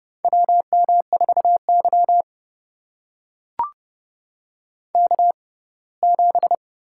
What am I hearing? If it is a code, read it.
Morse code, heard as a keyed tone, about 30 wpm, spaced as for 20 wpm, 700 Hz. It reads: WM4Y K7
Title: USA Call Signs - 4 Characters - 6x Character Spacing - 30wpm